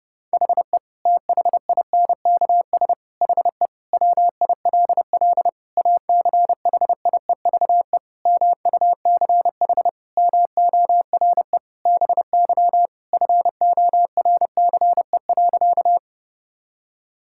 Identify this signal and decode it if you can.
Morse code, heard as a keyed tone, about 30 wpm, 700 Hz. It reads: HE THINKS HE WILL ACHIEVE MUCH MORE BY FORCE.
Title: Sentences from Top 1000 Words - Review Part 6 - 30wpm